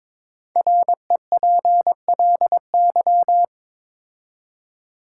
Morse code, heard as a keyed tone, about 22 wpm, 700 Hz. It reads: REPLY
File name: 5 Letter Words - First 3 Letters Given - 22wpm